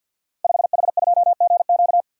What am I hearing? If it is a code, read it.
Morse code, heard as a keyed tone, about 48 wpm, 700 Hz. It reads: 5H2GQ
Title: Code Groups - 5 Letter:Numbers - 50wpm